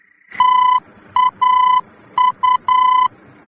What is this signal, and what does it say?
Morse code, heard as a keyed tone, about 10 wpm, 1010 Hz. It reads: TAU
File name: germany / mx / TAU-D-VOR